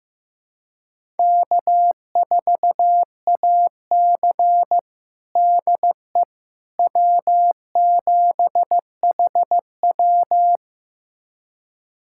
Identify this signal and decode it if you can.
Morse code, heard as a keyed tone, about 16 wpm, 700 Hz. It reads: K4AC DE W7HW
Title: USA Callsign DE Callsign - 4 Characters - 15wpm